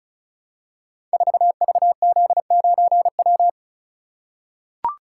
Morse code, heard as a keyed tone, about 36 wpm, 700 Hz. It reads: 4VZ9W
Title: Code Groups - 5 Letter:Numbers - 35wpm